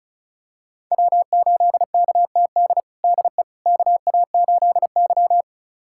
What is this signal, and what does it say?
Morse code, heard as a keyed tone, about 36 wpm, 700 Hz. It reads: W8KTD DE KA8Y
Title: USA Callsign DE Callsign 35wpm